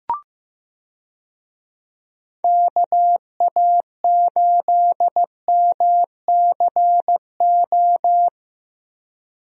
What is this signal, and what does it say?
Morse code, heard as a keyed tone, about 16 wpm, 700 Hz. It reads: KA8MCO
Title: USA Call Signs - 6 Characters - Repeated 3x - 15wpm